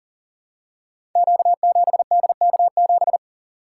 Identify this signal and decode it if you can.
Morse code, heard as a keyed tone, about 40 wpm, 700 Hz. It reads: Q7DK7
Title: Code Groups - 5 Letter:Numbers - 40wpm